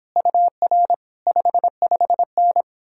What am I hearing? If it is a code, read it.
Morse code, heard as a keyed tone, about 31 wpm, 700 Hz. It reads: UR 55N